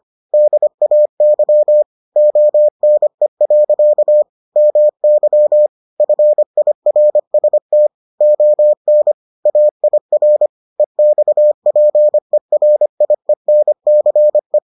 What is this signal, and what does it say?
Morse code, heard as a keyed tone, about 25 wpm, 600 Hz. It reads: DAY ONE. MY FIRST ON AIR EXPERIENCE